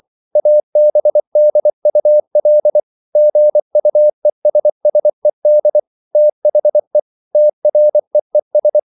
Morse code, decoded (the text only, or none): ABDUL GUESSED THE TREES